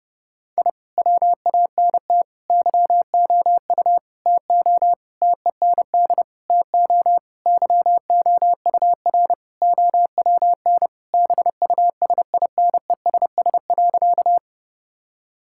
Morse code, heard as a keyed tone, about 30 wpm, 700 Hz. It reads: I WANT YOU TO TEND TO YOUR OWN BUSINESS.
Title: Sentences from Top 1000 Words - Review Part 5 - 30wpm